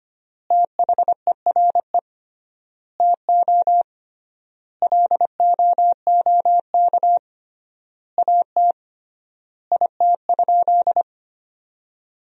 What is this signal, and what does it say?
Morse code, heard as a keyed tone, about 25 wpm, 700 Hz. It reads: THERE TO LOOK AT IT?
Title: 3x Word-Spacing - Sentences from Top 100 Words - 25wpm